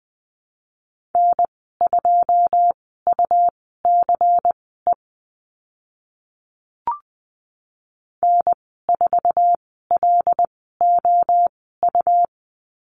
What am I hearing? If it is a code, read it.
Morse code, heard as a keyed tone, about 20 wpm, spaced as for 14 wpm, 700 Hz. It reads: N2UCE N4LOU
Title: 140 Lesson 23g - Letter f - Call Signs 20wpm